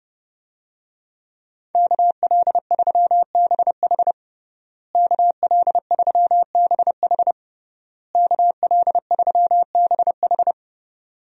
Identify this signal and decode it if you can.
Morse code, heard as a keyed tone, about 30 wpm, 700 Hz. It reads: KL3BH KL3BH KL3BH
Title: USA Call Signs - 5 Characters - Repeated 3x - 30wpm